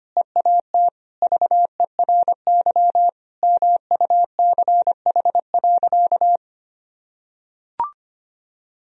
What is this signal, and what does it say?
Morse code, heard as a keyed tone, about 25 wpm, 700 Hz. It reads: EAT VERY MUCH.
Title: Sentences from Top 1000 Words - Review Part 1 - 25wpm